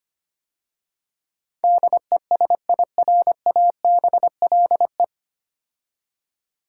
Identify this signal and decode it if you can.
Morse code, heard as a keyed tone, about 25 wpm, 700 Hz. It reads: DESIRABLE
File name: practice / 25wpm / 9 Letter Words - First 3 Letters Given - 25wpm